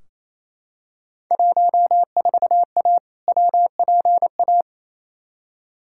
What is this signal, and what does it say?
Morse code, heard as a keyed tone, about 27 wpm, 700 Hz. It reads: 14A WPA
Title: Field Day Exchanges Encore - 28wpm